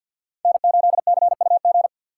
Morse code, heard as a keyed tone, about 52 wpm, 700 Hz. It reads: N8CUG
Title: USA Call Signs - 5 Characters - 6x Character Spacing - 50wpm